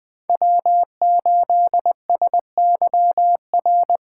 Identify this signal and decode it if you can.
Morse code, heard as a keyed tone, about 21 wpm, 700 Hz. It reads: W8SYR